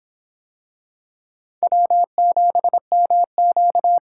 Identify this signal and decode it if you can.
Morse code, heard as a keyed tone, about 25 wpm, 700 Hz. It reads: W7MQ